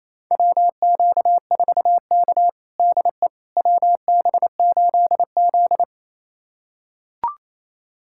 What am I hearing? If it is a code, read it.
Morse code, heard as a keyed tone, about 28 wpm, 700 Hz. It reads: WQ4K DE WB8Z
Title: USA Callsign DE Callsign - 4 Characters - 28wpm